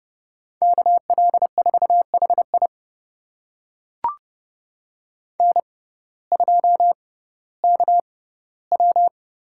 Morse code, heard as a keyed tone, about 32 wpm, spaced as for 17 wpm, 700 Hz. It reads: KL4HI N2KW
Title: USA Call Signs - 6x Character Spacing - 30wpm